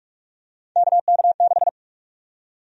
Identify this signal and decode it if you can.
Morse code, heard as a keyed tone, about 46 wpm, 700 Hz. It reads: KK6
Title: USA Call Sign Prefixes - 45wpm